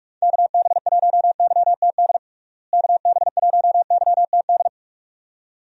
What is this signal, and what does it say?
Morse code, heard as a keyed tone, about 46 wpm, 700 Hz. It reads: KB1YTD KB1YTD
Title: USA Call Signs - 6 Characters - Repeated 2x - 45wpm